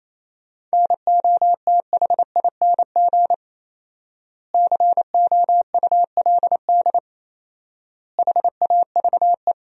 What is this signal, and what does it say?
Morse code, heard as a keyed tone, about 28 wpm, 700 Hz. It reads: NOTHING COULD HAVE
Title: 4x Word-Spacing - Sentences from Top 500 Words - 28wpm